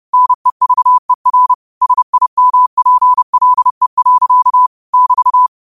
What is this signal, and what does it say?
Morse code, heard as a keyed tone, about 30 wpm, 1000 Hz. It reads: NEVER SIMPLE. <BT>
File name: qotdfiles / 30wpm